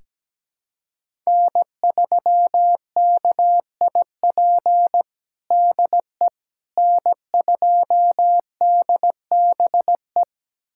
Morse code, heard as a keyed tone, about 17 wpm, 700 Hz. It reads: N3KIP DE N2DBE
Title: USA Callsign DE Callsign - 5 Characters - 17wpm